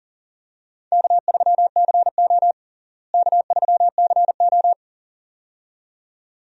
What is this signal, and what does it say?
Morse code, heard as a keyed tone, about 40 wpm, 700 Hz. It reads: K3CO K3CO
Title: USA Call Signs - 4 Characters - Repeated 2x - 40wpm